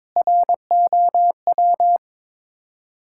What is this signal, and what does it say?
Morse code, heard as a keyed tone, about 22 wpm, 700 Hz. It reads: ROW
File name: practice / 22wpm / Sets of 7 Words - 22wpm